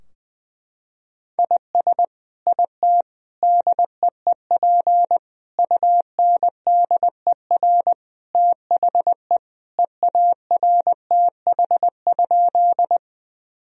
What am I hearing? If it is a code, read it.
Morse code, heard as a keyed tone, about 21 wpm, 700 Hz. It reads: IS IT DEEP UNDER THE EARTH?
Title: Sentences from Top 1000 Words - Review Part 3 - 20wpm